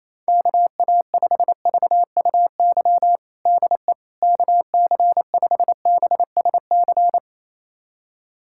Morse code, heard as a keyed tone, about 28 wpm, 700 Hz. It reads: KA5VUY DE KC5BSC
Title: USA Callsign DE Callsign - 6 Characters - 28wpm